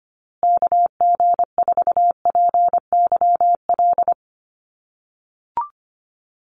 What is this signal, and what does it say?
Morse code, heard as a keyed tone, about 25 wpm, 700 Hz. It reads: KG4PYL
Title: USA Call Signs Repeated 2x 25wpm